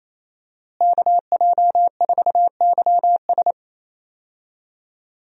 Morse code, heard as a keyed tone, about 28 wpm, 700 Hz. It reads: KJ4YS
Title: USA Call Signs - 5 Characters - Repeated 3x - 28wpm